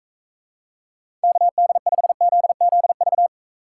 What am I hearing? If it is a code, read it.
Morse code, heard as a keyed tone, about 45 wpm, 700 Hz. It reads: KD5ZZV